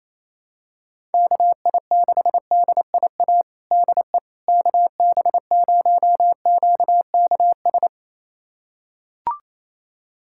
Morse code, heard as a keyed tone, about 28 wpm, 700 Hz. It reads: KI6DIA DE KB0QKS
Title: USA Callsign DE Callsign 28wpm